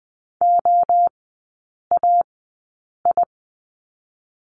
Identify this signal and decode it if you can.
Morse code, heard as a keyed tone, about 20 wpm, 700 Hz. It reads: O A I
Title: 029 Lesson 6e - Letter i - Three Characters 20wpm